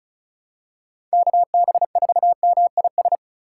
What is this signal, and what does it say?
Morse code, heard as a keyed tone, about 36 wpm, 700 Hz. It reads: KB4MIS
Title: USA Call Signs - 6 Characters - 2x Character Spacing - 35wpm